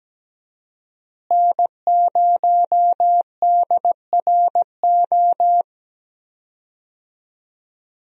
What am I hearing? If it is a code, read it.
Morse code, heard as a keyed tone, about 17 wpm, 700 Hz. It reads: N0DRO
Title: USA Call Signs - 5 Characters - Repeated 3x - 17wpm